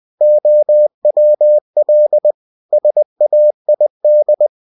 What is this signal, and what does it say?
Morse code, heard as a keyed tone, about 20 wpm, 600 Hz. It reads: OWL SAID
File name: cw-books / winnie-the-pooh / winnie-20wpm-page-0106